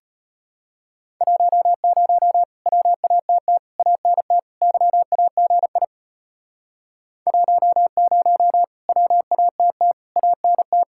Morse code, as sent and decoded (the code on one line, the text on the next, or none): .---- ----- / .-- .- - - / .- -. - / -.-- .- --. .. / .---- ----- / .-- .- - - / .- -. -
10 WATT ANT YAGI 10 WATT ANT